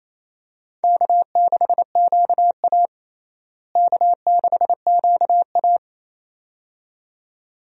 Morse code, heard as a keyed tone, about 28 wpm, 700 Hz. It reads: K6QA K6QA
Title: USA Call Signs - 4 Characters - Repeated 2x - 28wpm